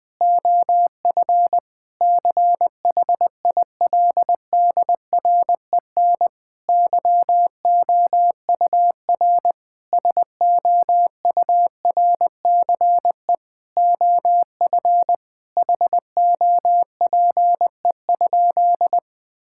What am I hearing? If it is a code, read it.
Morse code, heard as a keyed tone, about 20 wpm, 700 Hz. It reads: OF CHILDREN YOUR SOURCE OF HOPE?
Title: Sentences from Top 1000 Words - Review Part 5 - 20wpm